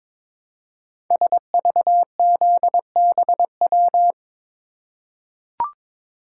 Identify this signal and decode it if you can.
Morse code, heard as a keyed tone, about 22 wpm, 700 Hz. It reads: SVZBW
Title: Code Groups - 5 Letters - 22wpm